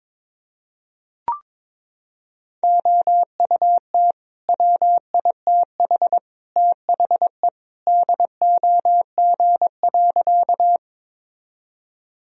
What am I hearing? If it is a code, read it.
Morse code, heard as a keyed tone, about 22 wpm, 700 Hz. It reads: OUT WITH THE DOG.
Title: Sentences from Top 1000 Words - Review Part 2 - 22wpm